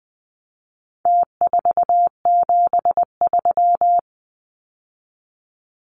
Morse code, heard as a keyed tone, about 20 wpm, 700 Hz. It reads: T473
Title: Numbers - Contest Style - 4-Digit 20wpm